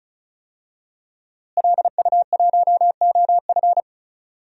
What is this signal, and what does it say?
Morse code, heard as a keyed tone, about 35 wpm, 700 Hz. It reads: LU1OF
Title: DXCC-100 Call Signs - 5 Characters - Repeated 3x - 35wpm